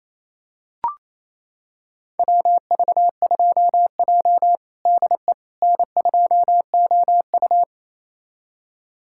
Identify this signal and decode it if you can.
Morse code, heard as a keyed tone, about 27 wpm, 690 Hz. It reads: WV2J DE N2OU
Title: USA Callsign DE Callsign - 4 Characters - 28wpm